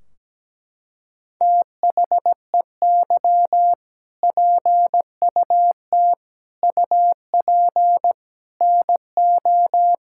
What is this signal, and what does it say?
Morse code, heard as a keyed tone, about 17 wpm, 700 Hz. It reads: THEY PUT UP NO
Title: Sentences from Top 1600 Words 17wpm